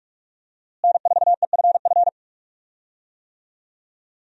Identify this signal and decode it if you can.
Morse code, heard as a keyed tone, about 46 wpm, 700 Hz. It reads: N4EFF
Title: USA Call Signs - 5 Characters - Repeated 3x - 45wpm